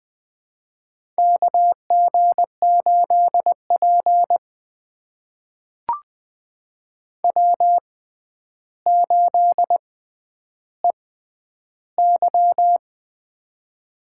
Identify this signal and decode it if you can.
Morse code, heard as a keyed tone, about 20 wpm, spaced as for 10 wpm, 700 Hz. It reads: KG8P W8EY
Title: USA Call Signs - 4 Characters - 6x Character Spacing - 20wpm